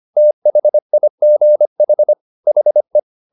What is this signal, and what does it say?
Morse code, heard as a keyed tone, about 25 wpm, 600 Hz. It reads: THIGH HE
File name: the old man and the sea part 1 25wpm